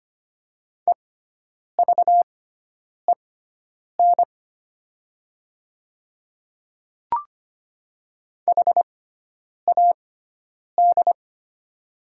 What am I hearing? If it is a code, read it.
Morse code, heard as a keyed tone, about 25 wpm, spaced as for 9 wpm, 700 Hz. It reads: EVEN HAD
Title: 6x Character Spacing - Top 200 Words - 25wpm